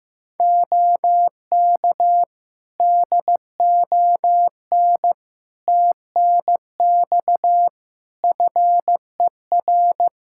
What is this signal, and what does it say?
Morse code, heard as a keyed tone, about 16 wpm, 700 Hz. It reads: OK DON TNX FER
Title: Practice QSO - Exchange 4 - Ending - 15wpm